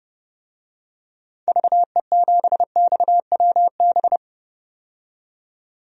Morse code, heard as a keyed tone, about 29 wpm, 700 Hz. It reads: VE7XWB
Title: DXCC-100 Call Signs - 6 Characters - 30wpm